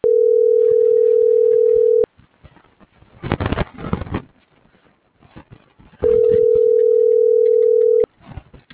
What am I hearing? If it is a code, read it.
Ringback tone (North America), 440 + 480 Hz.